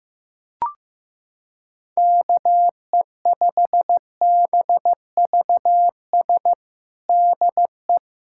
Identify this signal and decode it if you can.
Morse code, heard as a keyed tone, about 15 wpm, 700 Hz. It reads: KE5BVS DE
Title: USA Callsign DE Callsign - 6 Characters - 15wpm